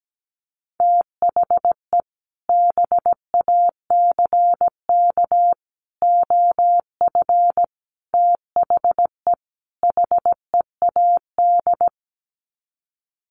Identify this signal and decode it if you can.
Morse code, heard as a keyed tone, about 17 wpm, 700 Hz. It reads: THE BACK OF THE HEAD
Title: Sets of 5 Words Encore2 17wpm